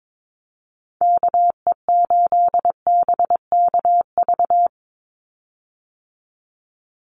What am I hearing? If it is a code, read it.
Morse code, heard as a keyed tone, about 22 wpm, 700 Hz. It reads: KE8BKV
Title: USA Call Signs 22wpm